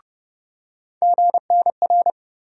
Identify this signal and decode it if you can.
Morse code, heard as a keyed tone, about 31 wpm, 700 Hz. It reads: GNR
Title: USA Call Sign Postfixes - 3 Letters 30wpm